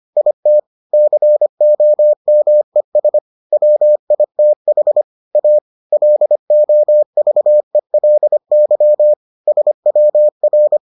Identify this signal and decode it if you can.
Morse code, heard as a keyed tone, about 25 wpm, 600 Hz. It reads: IT COMES WITH A LOVELY SWR